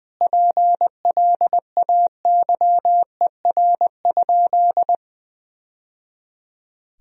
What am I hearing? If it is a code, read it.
Morse code, heard as a keyed tone, about 21 wpm, 700 Hz. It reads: PLAYER?
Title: Sentences from Top 1000 Words - Review Part 4 - 20wpm